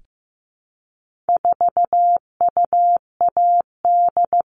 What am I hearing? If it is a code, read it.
Morse code, heard as a keyed tone, about 15 wpm, 700 Hz. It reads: 4UAD
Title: All Cut Numbers - 4-Digit 15wpm